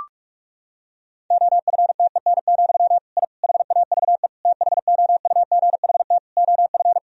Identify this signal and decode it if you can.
Morse code, heard as a keyed tone, about 46 wpm, 700 Hz. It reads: OFTEN, I HAVE THOUGHT OF
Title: Sentences from Top 1000 Words - Review Part 1 - 45wpm